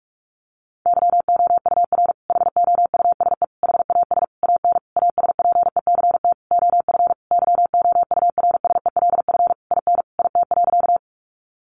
Morse code, heard as a keyed tone, about 44 wpm, 700 Hz. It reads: YOUR HOUSE HAS AN ASPECT OF YOURSELF IN IT.
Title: Sentences from Top 1300 Words 45wpm